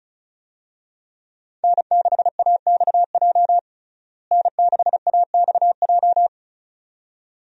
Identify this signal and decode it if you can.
Morse code, heard as a keyed tone, about 36 wpm, 700 Hz. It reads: N6AXJ N6AXJ
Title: USA Call Signs - 5 Characters - Repeated 2x - 35wpm